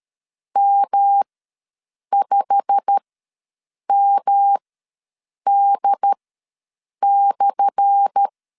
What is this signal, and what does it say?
Morse code, heard as a keyed tone, about 13 wpm, spaced as for 7 wpm, 800 Hz. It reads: M5MD/
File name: LEZIONI / 10 SIGNS / 10A SLASH ws 12wpm 3443